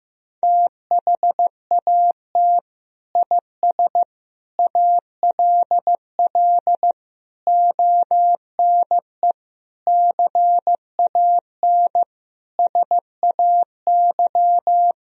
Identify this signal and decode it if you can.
Morse code, heard as a keyed tone, about 15 wpm, 700 Hz. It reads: THAT IS ALL ONE CAN SAY
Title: Sentences from Top 200 Words and Abbreviations 15wpm